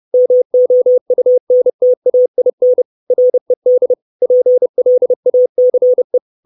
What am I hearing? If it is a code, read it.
Morse code, heard as a keyed tone, about 30 wpm, 500 Hz. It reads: MOUNTAIN RED PLACE
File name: 632 Words 30WPM